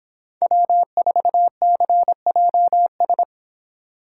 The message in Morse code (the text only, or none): W4CJS